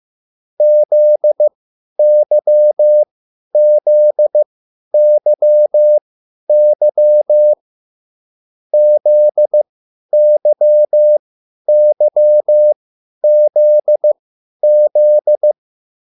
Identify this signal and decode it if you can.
Morse code, heard as a keyed tone, about 15 wpm, spaced as for 10 wpm, 600 Hz. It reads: ZYZYY ZYYZZ
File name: QSL / CW lesson